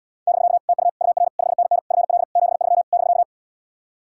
Morse code, heard as a keyed tone, about 80 wpm, 700 Hz. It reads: <BK> TU UR 5NN FL 73 <BK>